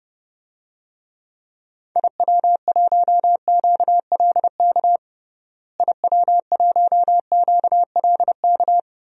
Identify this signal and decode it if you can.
Morse code, heard as a keyed tone, about 32 wpm, 700 Hz. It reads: IW1QLK IW1QLK
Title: DXCC-100 Call Signs - 6 Characters - Repeated 2x - 30wpm